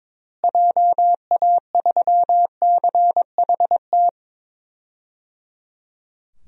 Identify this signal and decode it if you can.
Morse code, heard as a keyed tone, about 22 wpm, 700 Hz. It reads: JA3CHT
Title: DXCC-100 Call Signs - 6 Characters - Repeated 3x - 22wpm